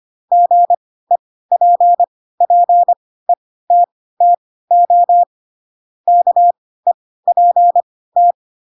Morse code, heard as a keyed tone, about 25 wpm, spaced as for 16 wpm, 700 Hz. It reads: GEPPETTO KEPT
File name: Pinocchio / Pinocchio ch3 10 25 ik1hge